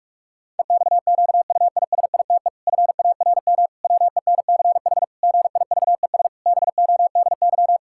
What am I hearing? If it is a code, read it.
Morse code, heard as a keyed tone, about 41 wpm, 700 Hz. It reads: EXQUISITE FARM WENCH GIVES BODY
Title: Pangram Challenge v2 - 45wpm